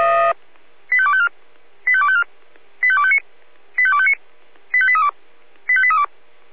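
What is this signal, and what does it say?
Five-tone selective call (ZVEI), heard as quick runs of five tones, tones of about 70 ms. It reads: 85254 85254 85259 85259 85832 85832
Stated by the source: ZVEI five-tone call received by SDR.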